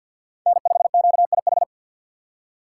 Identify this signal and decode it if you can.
Morse code, heard as a keyed tone, about 47 wpm, 690 Hz. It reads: N5QIH
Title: USA Call Signs - 5 Characters - 6x Character Spacing - 50wpm